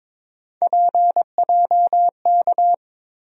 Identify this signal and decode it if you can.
Morse code, heard as a keyed tone, about 22 wpm, 700 Hz. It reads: PJK
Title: USA Call Sign Postfixes - 3 Letters 22wpm